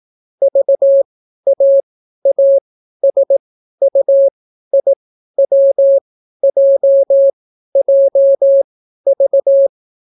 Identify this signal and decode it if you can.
Morse code, heard as a keyed tone, about 18 wpm, spaced as for 12 wpm, 550 Hz. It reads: VAASUIWJJV